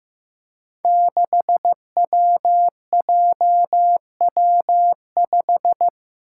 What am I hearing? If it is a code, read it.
Morse code, heard as a keyed tone, about 15 wpm, 700 Hz. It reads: 6WJW5
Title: Five Challenging Letter-Number Combinations 15wpm